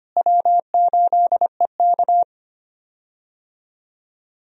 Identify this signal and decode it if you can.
Morse code, heard as a keyed tone, about 25 wpm, 700 Hz. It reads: W8EK
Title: USA Call Signs - 4 Characters - Repeated 2x - 25wpm